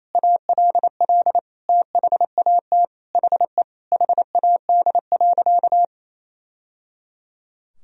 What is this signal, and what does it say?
Morse code, heard as a keyed tone, about 28 wpm, 700 Hz. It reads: ALL THAT HE HAD.
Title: Sentences from Top 1000 Words - Review Part 2 - 28wpm